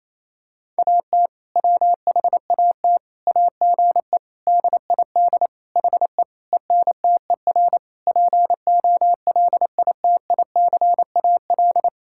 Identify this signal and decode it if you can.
Morse code, heard as a keyed tone, about 27 wpm, 700 Hz. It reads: AT WHAT AGE DID HE ENTER POLITICAL
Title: Sentences from Top 1000 Words - Review Part 5 - 28wpm